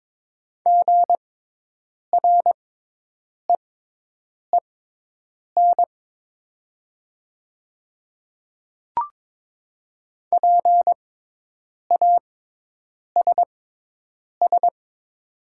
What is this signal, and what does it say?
Morse code, heard as a keyed tone, about 22 wpm, spaced as for 8 wpm, 690 Hz. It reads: GREEN PASS
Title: 6x Character Spacing - Top 500 Words - 22wpm